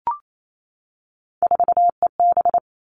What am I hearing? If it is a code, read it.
Morse code, heard as a keyed tone, about 28 wpm, 700 Hz. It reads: 4EB
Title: All Cut Numbers - 3-Digit 28wpm